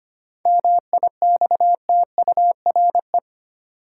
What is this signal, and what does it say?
Morse code, heard as a keyed tone, about 25 wpm, 700 Hz. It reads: MIXTURE
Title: Sentences from Top 1500 Words - New Word Drill - 25wpm